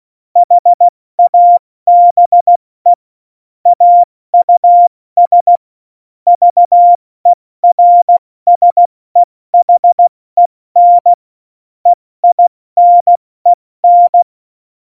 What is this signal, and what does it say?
Morse code, heard as a keyed tone, about 16 wpm, 700 Hz. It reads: HABE AUS VERSEHEN EINEN
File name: Morsekurs / Witze / Witz 20 March 2018, 70 F80